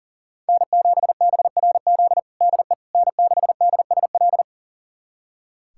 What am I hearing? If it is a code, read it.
Morse code, heard as a keyed tone, about 37 wpm, 700 Hz. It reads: N7BRZ DE N6DSL
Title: USA Callsign DE Callsign - 5 Characters - 40wpm